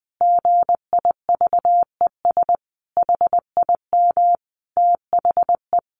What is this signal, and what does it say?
Morse code, heard as a keyed tone, about 19 wpm, 700 Hz. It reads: GIVES HIM THE
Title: Easy Phrases 20wpm